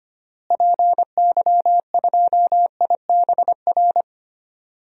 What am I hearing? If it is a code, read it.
Morse code, heard as a keyed tone, about 25 wpm, 700 Hz. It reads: PY2IBR
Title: DXCC-100 Call Signs - 6 Characters - Repeated 2x - 25wpm